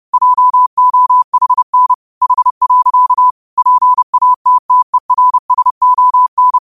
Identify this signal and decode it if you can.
Morse code, heard as a keyed tone, about 30 wpm, 1000 Hz. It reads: JOHN H. PATTERSON
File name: qotdfiles / 30wpm